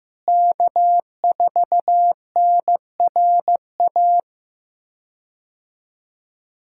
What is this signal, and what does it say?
Morse code, heard as a keyed tone, about 16 wpm, 700 Hz. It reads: K4NRA
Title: K1USN SST QSO Practice - 15wpm